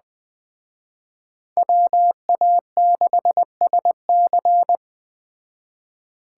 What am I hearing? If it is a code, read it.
Morse code, heard as a keyed tone, about 20 wpm, 700 Hz. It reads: WA6SC
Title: Speed Racing - USA 2x2 Call Signs 17wpm